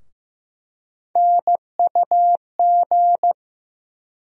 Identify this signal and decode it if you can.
Morse code, heard as a keyed tone, about 15 wpm, 700 Hz. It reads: NUG
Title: USA Call Sign Postfixes - 3 Letters 15wpm